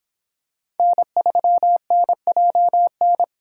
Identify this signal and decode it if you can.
Morse code, heard as a keyed tone, about 26 wpm, 700 Hz. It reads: N3NJN